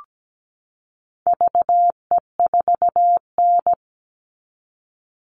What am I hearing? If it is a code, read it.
Morse code, heard as a keyed tone, about 17 wpm, 700 Hz. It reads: VE4N
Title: All Cut Numbers - 4-Digit 17wpm